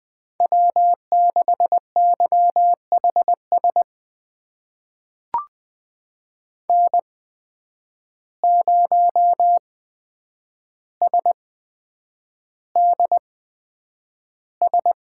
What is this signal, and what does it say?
Morse code, heard as a keyed tone, about 20 wpm, spaced as for 9 wpm, 700 Hz. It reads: W6YHS N0SDS
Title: USA Call Signs - 5 Characters - 8x Character Spacing - 20wpm